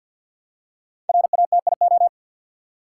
Morse code, heard as a keyed tone, about 52 wpm, 700 Hz. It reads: RATIO
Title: Five Letter Words - 50wpm